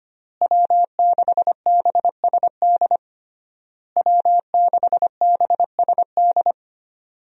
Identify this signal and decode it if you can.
Morse code, heard as a keyed tone, about 25 wpm, 700 Hz. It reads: W6BSD W6BSD
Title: USA Call Signs - 5 Characters - Repeated 3x - 25wpm